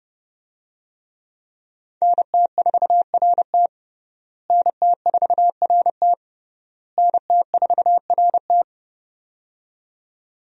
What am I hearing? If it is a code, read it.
Morse code, heard as a keyed tone, about 30 wpm, 700 Hz. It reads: NT4RT NT4RT NT4RT
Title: USA Call Signs - 5 Characters - Repeated 3x - 30wpm